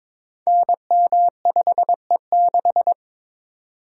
Morse code, heard as a keyed tone, about 22 wpm, 700 Hz. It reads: NM5E6
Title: Code Groups - 5 Letter:Numbers - 22wpm